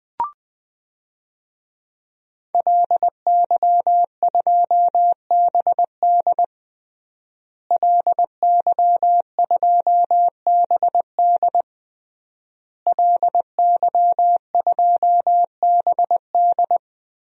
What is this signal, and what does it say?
Morse code, heard as a keyed tone, about 21 wpm, 700 Hz. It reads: LY2BD LY2BD LY2BD
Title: DXCC-100 Call Signs - 5 Characters - Repeated 3x - 20wpm